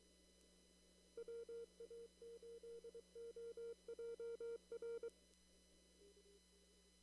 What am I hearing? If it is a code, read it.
Morse code, heard as a keyed tone, about 23 wpm, 460 Hz. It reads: WA8OJR
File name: CW / kt5j wpx cw-op-k5pi